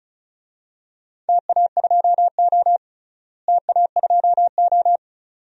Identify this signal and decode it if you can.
Morse code, heard as a keyed tone, about 34 wpm, 700 Hz. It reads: TA2O TA2O
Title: DXCC-100 Call Signs - 4 Characters - Repeated 2x - 35wpm